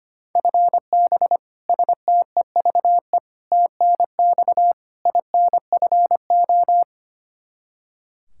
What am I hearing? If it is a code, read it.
Morse code, heard as a keyed tone, about 25 wpm, 700 Hz. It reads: FB STEVE TNX INFO
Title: Practice QSO - Exchange 3 - More Info - 25wpm